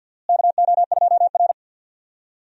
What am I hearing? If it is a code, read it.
Morse code, heard as a keyed tone, about 52 wpm, 700 Hz. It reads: KO2R
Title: USA Call Signs - 4 Characters - Repeated 3x - 50wpm